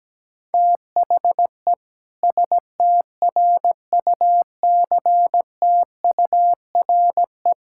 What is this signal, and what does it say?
Morse code, heard as a keyed tone, about 17 wpm, 700 Hz. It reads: THE STRUCTURE
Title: Sentences from Top 1000 Words - Review Part 6 - 17wpm